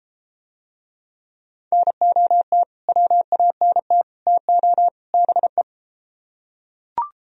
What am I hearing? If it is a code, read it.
Morse code, heard as a keyed tone, about 31 wpm, 700 Hz. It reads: NOT WANT TO BE